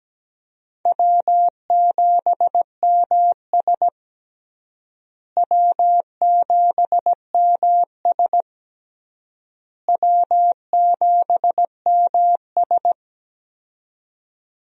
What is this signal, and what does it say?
Morse code, heard as a keyed tone, about 17 wpm, 700 Hz. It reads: W7MS W7MS W7MS
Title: USA Call Signs - 4 Characters - Repeated 3x - 17wpm